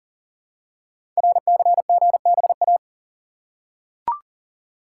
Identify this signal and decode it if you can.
Morse code, heard as a keyed tone, about 40 wpm, 700 Hz. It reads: RCGBA
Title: Code Groups - 5 Letters - 40wpm